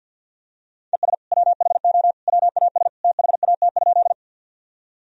Morse code, heard as a keyed tone, about 48 wpm, 700 Hz. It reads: ES WHO WAS THAT?
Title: Sentences from Top 100 Words and Abbreviations 50wpm